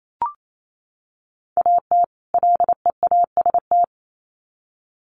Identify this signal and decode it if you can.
Morse code, heard as a keyed tone, about 28 wpm, 700 Hz. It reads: AT LEAST
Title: Sets of 2 Words 28wpm